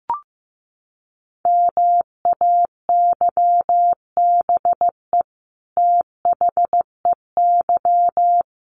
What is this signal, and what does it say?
Morse code, heard as a keyed tone, about 15 wpm, 700 Hz. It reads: MAYBE THEY
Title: Sentences from Top 1000 Words 15wpm